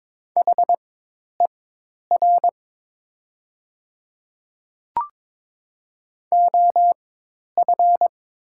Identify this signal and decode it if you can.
Morse code, heard as a keyed tone, about 22 wpm, spaced as for 10 wpm, 700 Hz. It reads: HER OF
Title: 4x Character Spacing - Top 300 Words - 22wpm